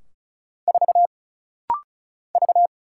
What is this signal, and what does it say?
Morse code, heard as a keyed tone, about 36 wpm, 700 Hz. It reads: 4 V
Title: Single Letter-Number - Rapid-Fire 35wpm